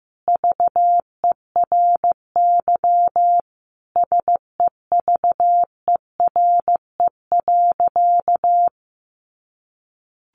Morse code, heard as a keyed tone, about 15 wpm, 700 Hz. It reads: VERY SEVERE.
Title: Sentences from Top 1400 Words 15wpm